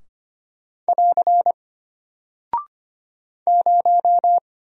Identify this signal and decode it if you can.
Morse code, heard as a keyed tone, about 25 wpm, 700 Hz. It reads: <AR> 0
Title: Single Character (Letter:Number:Punctuation:Prosign) Alt 25wpm